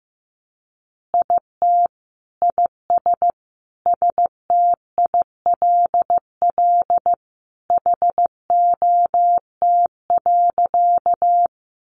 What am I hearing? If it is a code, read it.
Morse code, heard as a keyed tone, about 15 wpm, 700 Hz. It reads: IT IS STILL HOT.
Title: Easy Phrases 15wpm